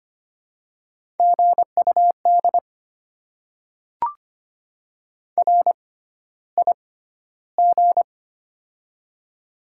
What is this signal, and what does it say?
Morse code, heard as a keyed tone, about 25 wpm, spaced as for 13 wpm, 690 Hz. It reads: GUD RIG
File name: Top 20 QSO Elements - 6x Character Spacing - 25wpm